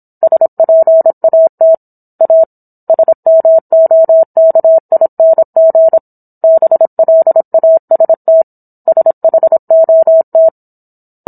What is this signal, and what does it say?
Morse code, heard as a keyed tone, about 27 wpm, 650 Hz. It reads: SPAT A SMOKING BLAST SHOT